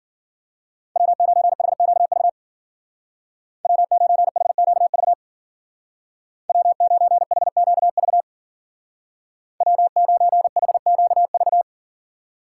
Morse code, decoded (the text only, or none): W9HQV W9HQV W9HQV W9HQV